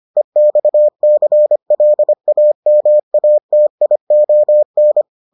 Morse code, heard as a keyed tone, about 25 wpm, 600 Hz. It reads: EXCLAMATION